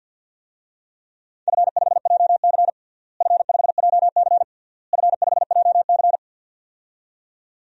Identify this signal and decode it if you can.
Morse code, heard as a keyed tone, about 48 wpm, 700 Hz. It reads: F5JC F5JC F5JC
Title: DXCC-100 Call Signs - 4 Characters - Repeated 3x - 50wpm